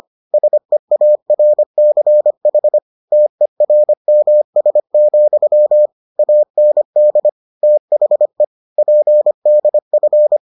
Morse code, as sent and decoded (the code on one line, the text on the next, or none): ... .- .-. -.-. .... / - . .-. -- ... --..-- / .- -. -.. / - .... / .--. -.. ..-.
SEARCH TERMS, AND THE PDF